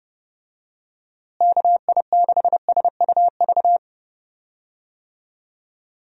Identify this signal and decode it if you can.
Morse code, heard as a keyed tone, about 30 wpm, 700 Hz. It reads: KI6SUV
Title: USA Call Signs - 6 Characters - Repeated 2x - 30wpm